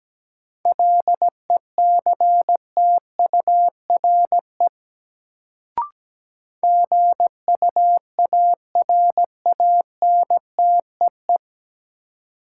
Morse code, heard as a keyed tone, about 17 wpm, 700 Hz. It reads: LECTURE GUARANTEE
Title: Sentences from Top 1600 Words - New Word Drill - 17wpm